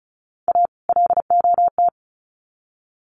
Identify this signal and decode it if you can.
Morse code, heard as a keyed tone, about 34 wpm, 700 Hz. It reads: A LOT
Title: Sets of 2 Words 35wpm